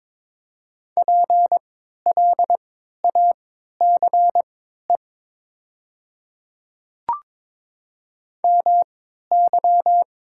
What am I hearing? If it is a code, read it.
Morse code, heard as a keyed tone, about 22 wpm, spaced as for 13 wpm, 700 Hz. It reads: PLACE MY
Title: 3x Character Spacing - Top 200 Words - 22wpm